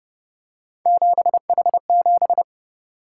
Morse code, heard as a keyed tone, about 29 wpm, 700 Hz. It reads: 7H7